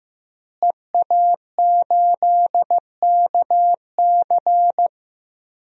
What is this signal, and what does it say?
Morse code, heard as a keyed tone, about 15 wpm, 700 Hz. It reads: EA8KC
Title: DXCC-100 Call Signs - 5 Characters - Repeated 2x - 15wpm